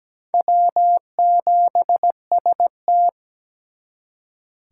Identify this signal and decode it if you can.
Morse code, heard as a keyed tone, about 20 wpm, 700 Hz. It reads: W7ST
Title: Speed Racing - USA Call Signs 17wpm